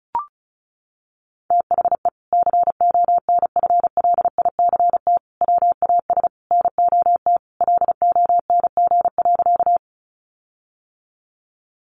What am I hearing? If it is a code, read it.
Morse code, heard as a keyed tone, about 34 wpm, 700 Hz. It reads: THE CONFLICT WAS NOT LONG.
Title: Sentences from Top 1300 Words 35wpm